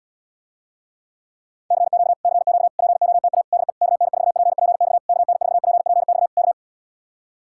Morse code, heard as a keyed tone, about 77 wpm, 700 Hz. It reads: CQ CQ POTA DE KN4PPX KN4PPX K